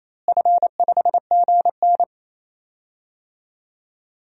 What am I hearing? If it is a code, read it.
Morse code, heard as a keyed tone, about 28 wpm, 700 Hz. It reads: F5GN
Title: DXCC-100 Call Signs - 4 Characters - Repeated 3x - 28wpm